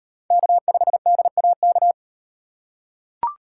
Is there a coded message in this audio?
Yes: Morse code from 0.3 to 1.9 s, heard as a keyed tone, about 38 wpm, 700 Hz. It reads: K5DAK